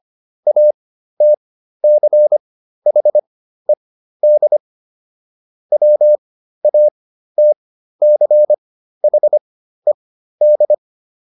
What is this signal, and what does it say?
Morse code, heard as a keyed tone, about 25 wpm, spaced as for 13 wpm, 600 Hz. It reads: ATCHED WATCHED